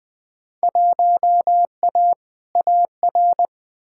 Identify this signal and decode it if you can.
Morse code, heard as a keyed tone, about 21 wpm, 700 Hz. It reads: 1A AR
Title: Field Day Exchanges Encore - 20wpm